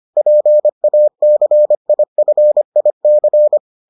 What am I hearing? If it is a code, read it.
Morse code, heard as a keyed tone, about 25 wpm, 600 Hz. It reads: PACIFIC